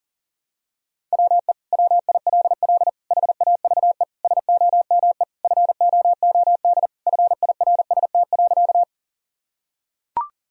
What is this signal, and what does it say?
Morse code, heard as a keyed tone, about 40 wpm, 700 Hz. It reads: WE WILL HAVE SOME FOOD FIRST.
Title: Sentences from Top 1000 Words - Review Part 1 - 40wpm